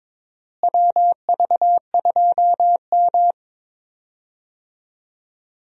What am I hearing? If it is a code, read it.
Morse code, heard as a keyed tone, about 22 wpm, 700 Hz. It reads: WV2M
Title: USA Call Signs - 4 Characters - Repeated 2x - 22wpm